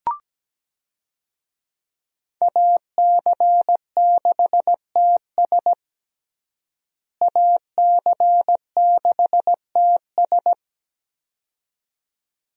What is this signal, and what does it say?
Morse code, heard as a keyed tone, about 17 wpm, 700 Hz. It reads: AC6TS AC6TS
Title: USA Call Signs - 5 Characters - Repeated 2x - 17wpm